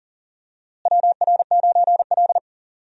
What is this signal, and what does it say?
Morse code, heard as a keyed tone, about 40 wpm, 700 Hz. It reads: WR9L
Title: USA Call Signs - 4 Characters - Repeated 3x - 40wpm